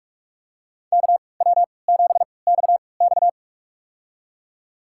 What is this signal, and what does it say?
Morse code, heard as a keyed tone, about 45 wpm, spaced as for 25 wpm, 700 Hz. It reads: KW7XX
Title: USA Call Signs - 5 Characters - 3x Character Spacing - 45wpm